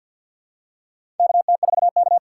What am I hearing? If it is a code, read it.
Morse code, heard as a keyed tone, about 49 wpm, 700 Hz. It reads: KT4K
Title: USA Call Signs - 4 Characters - 2x Character Spacing - 50wpm